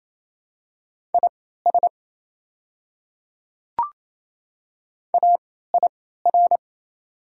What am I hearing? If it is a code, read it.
Morse code, heard as a keyed tone, about 28 wpm, spaced as for 16 wpm, 690 Hz. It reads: IS AIR
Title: 3x Character Spacing - Top 200 Words - 28wpm